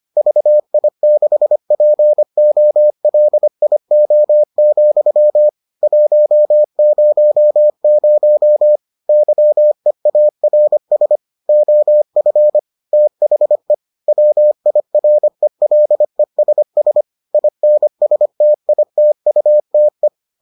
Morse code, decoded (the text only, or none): VI6POLIO, 100 YEARS OF THE WIRELESS INSTITUTE